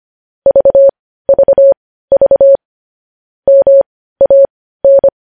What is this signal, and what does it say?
Morse code, heard as a keyed tone, about 25 wpm, spaced as for 15 wpm, 550 Hz. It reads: VVV MAN
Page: news headlines in morse code at 15 wpm